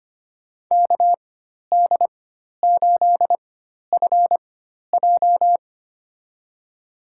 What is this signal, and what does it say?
Morse code, heard as a keyed tone, about 25 wpm, spaced as for 12 wpm, 700 Hz. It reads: KD8FJ
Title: USA Call Signs - 5 Characters - 4x Character Spacing - 25wpm